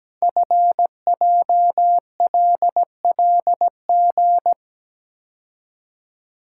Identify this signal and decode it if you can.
Morse code, heard as a keyed tone, about 17 wpm, 700 Hz. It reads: FJLLG
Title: Code Groups - 5 Letters - 17wpm